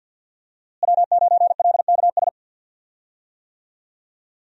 Morse code, heard as a keyed tone, about 47 wpm, 700 Hz. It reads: W9LKS
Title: USA Call Signs - 5 Characters - Repeated 2x - 50wpm